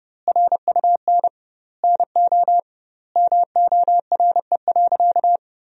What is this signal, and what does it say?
Morse code, heard as a keyed tone, about 30 wpm, 700 Hz. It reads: RUN NO MORE.
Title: 2x Word-Spacing - Sentences from Top 300 Words - 30wpm